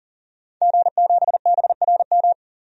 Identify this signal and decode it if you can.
Morse code, heard as a keyed tone, about 40 wpm, 700 Hz. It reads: G7BRM
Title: DXCC-100 Call Signs - 5 Characters - 40wpm